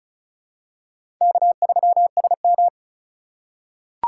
Morse code, heard as a keyed tone, about 36 wpm, 700 Hz. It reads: K3SM
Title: USA Call Signs - 2x Character Spacing - 35wpm